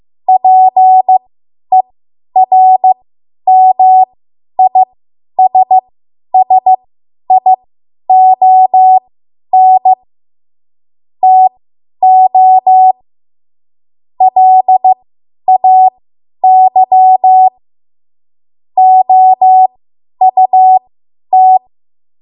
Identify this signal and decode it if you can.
Morse code, heard as a keyed tone, about 15 wpm, spaced as for 10 wpm, 750 Hz. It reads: PERMISSION TO LAY OUT
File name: Archive / 10 WPM / 10WPM